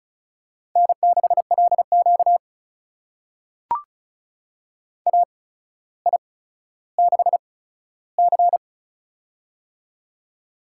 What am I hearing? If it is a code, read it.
Morse code, heard as a keyed tone, about 33 wpm, spaced as for 15 wpm, 700 Hz. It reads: N6LQ AI6C
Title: USA Call Signs - 4 Characters - 8x Character Spacing - 35wpm